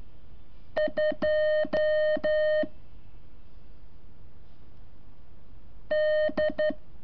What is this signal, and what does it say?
Morse code, heard as a keyed tone, about 12 wpm, 610 Hz. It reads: 2 D
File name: curso / ejercicio 24